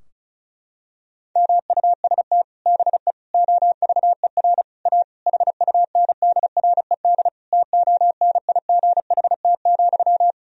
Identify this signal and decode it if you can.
Morse code, heard as a keyed tone, about 34 wpm, 700 Hz. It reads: MUST BE OVER A HUNDRED TONIGHT,
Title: Sentences from Top 1000 Words - Review Part 2 - 35wpm